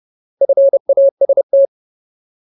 Morse code, heard as a keyed tone, about 30 wpm, 550 Hz. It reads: FAST
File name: Ultimate Word List / 30WPM Ultimate